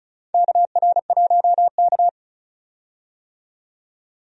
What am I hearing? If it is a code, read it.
Morse code, heard as a keyed tone, about 34 wpm, 700 Hz. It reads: KR1K
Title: USA Call Signs - 4 Characters - 35wpm